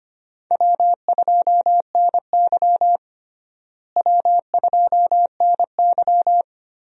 Morse code, heard as a keyed tone, about 25 wpm, 700 Hz. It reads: W2NY W2NY
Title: USA Call Signs - 4 Characters - Repeated 3x - 25wpm